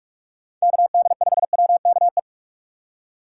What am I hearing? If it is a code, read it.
Morse code, heard as a keyed tone, about 42 wpm, 690 Hz. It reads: KD5WKE
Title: USA Call Signs - 6 Characters - 3x Character Spacing - 45wpm